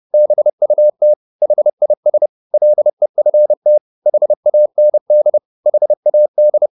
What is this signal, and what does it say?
Morse code, heard as a keyed tone, about 30 wpm, 600 Hz. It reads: BUT HIS LEFT HAND HAD
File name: the old man and the sea 2 30wpm